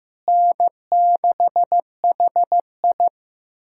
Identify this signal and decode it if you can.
Morse code, heard as a keyed tone, about 15 wpm, 700 Hz. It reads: N6HI
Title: K1USN SST QSO Practice - 15wpm